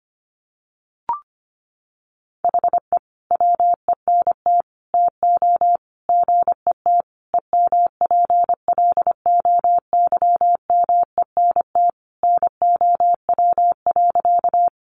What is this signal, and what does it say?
Morse code, heard as a keyed tone, about 25 wpm, 700 Hz. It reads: HE WENT TO GET EMPLOYMENT NOW.
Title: Sentences from Top 1500 Words 25wpm